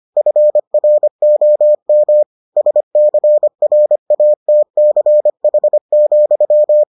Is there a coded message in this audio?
Yes: Morse code from 0.2 to 6.8 s, heard as a keyed tone, about 25 wpm, 600 Hz. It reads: FROM SCRATCH,